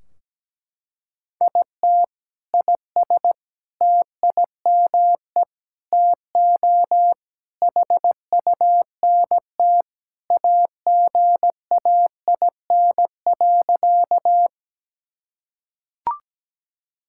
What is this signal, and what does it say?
Morse code, heard as a keyed tone, about 17 wpm, 700 Hz. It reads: IT IS TIME TO HUNT AGAIN.
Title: Sentences from Top 1600 Words 17wpm